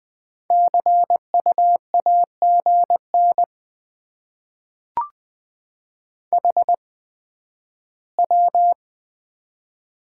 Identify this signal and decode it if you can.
Morse code, heard as a keyed tone, about 20 wpm, spaced as for 12 wpm, 700 Hz. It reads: CU AGN HW
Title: Top 20 QSO Elements - 8x Character Spacing - 20wpm